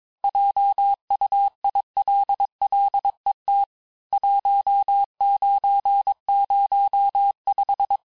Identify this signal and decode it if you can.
Morse code, heard as a keyed tone, about 22 wpm, 800 Hz. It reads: JUILLET 1905
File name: texte clair vitesse 16 puis 21 mpm